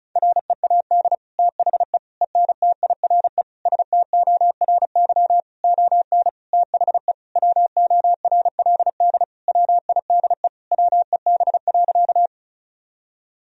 Morse code, heard as a keyed tone, about 36 wpm, 700 Hz. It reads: READ THE ENTIRE STORY ON THE WORLD WIDE WEB.
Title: Sentences from Top 1000 Words - Review Part 4 - 35wpm